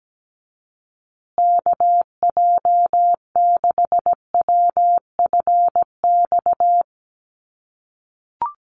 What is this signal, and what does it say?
Morse code, heard as a keyed tone, about 17 wpm, 700 Hz. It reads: KJ6WFX
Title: USA Call Signs Repeated 3x 17wpm